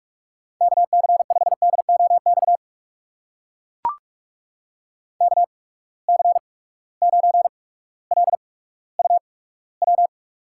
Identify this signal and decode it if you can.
Morse code, heard as a keyed tone, about 44 wpm, spaced as for 19 wpm, 700 Hz. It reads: KC5DOX KC9LUW
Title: USA Call Signs - 6 Characters - 8x Character Spacing - 45wpm